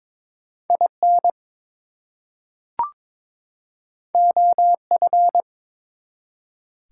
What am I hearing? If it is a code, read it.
Morse code, heard as a keyed tone, about 22 wpm, 700 Hz. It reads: IN OF
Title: Two Letter Words - 22wpm